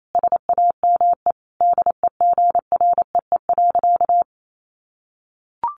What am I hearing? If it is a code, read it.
Morse code, heard as a keyed tone, about 28 wpm, 700 Hz. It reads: SAME DEGREE.
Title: Sentences from Top 1500 Words 28wpm